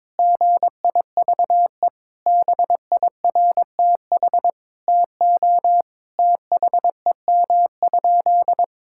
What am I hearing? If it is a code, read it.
Morse code, heard as a keyed tone, about 22 wpm, 700 Hz. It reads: GIVE BIRTH TO THEM?
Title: Sentences from Top 1000 Words - Review Part 5 - 22wpm